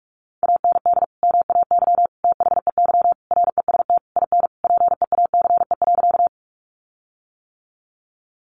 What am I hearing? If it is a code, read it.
Morse code, heard as a keyed tone, about 44 wpm, 700 Hz. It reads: AND MAY THEY REST IN PEACE.
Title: Sentences from Top 1500 Words 45wpm